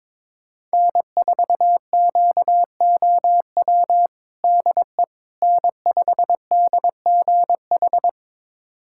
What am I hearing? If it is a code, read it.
Morse code, heard as a keyed tone, about 22 wpm, 700 Hz. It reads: N4QOW DE N5DGH
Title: USA Callsign DE Callsign - 5 Characters - 22wpm